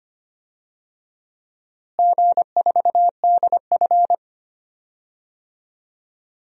Morse code, heard as a keyed tone, about 25 wpm, 700 Hz. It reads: G4DF
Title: DXCC-100 Call Signs - 4 Characters - Repeated 2x - 25wpm